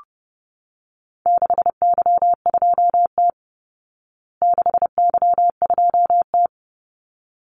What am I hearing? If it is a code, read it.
Morse code, heard as a keyed tone, about 30 wpm, 700 Hz. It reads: 6Y2T 6Y2T
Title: Call Signs Repeated 3x - 30wpm